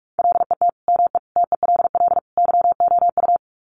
Morse code, heard as a keyed tone, about 46 wpm, 700 Hz. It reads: LET ME TELL YOU
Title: Sets of 4 Words 45wpm